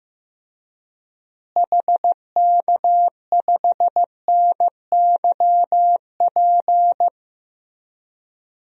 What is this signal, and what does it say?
Morse code, heard as a keyed tone, about 15 wpm, 700 Hz. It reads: HK5NYP
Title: DXCC-100 Call Signs - 6 Characters - Repeated 2x - 15wpm